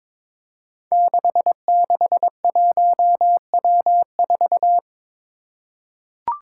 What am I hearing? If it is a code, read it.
Morse code, heard as a keyed tone, about 22 wpm, 700 Hz. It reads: 661W4
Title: Five Challenging Letter-Number Combinations 22wpm